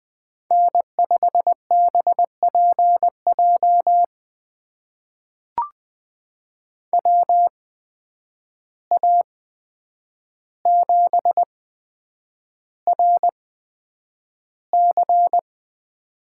Morse code, heard as a keyed tone, about 20 wpm, spaced as for 9 wpm, 700 Hz. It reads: N5BPJ WA7RC
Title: USA Call Signs - 5 Characters - 8x Character Spacing - 20wpm